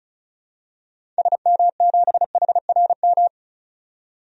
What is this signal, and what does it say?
Morse code, heard as a keyed tone, about 36 wpm, 690 Hz. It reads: SM7HRM
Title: DXCC-100 Call Signs - 6 Characters - 35wpm